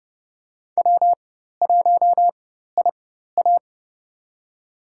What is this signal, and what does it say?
Morse code, heard as a keyed tone, about 31 wpm, spaced as for 14 wpm, 700 Hz. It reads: W1IA
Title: USA Call Signs - 4 Characters - 4x Character Spacing - 30wpm